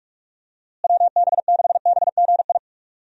Morse code, heard as a keyed tone, about 46 wpm, 700 Hz. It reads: WB6BGI
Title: USA Call Signs - 6 Characters - Repeated 2x - 45wpm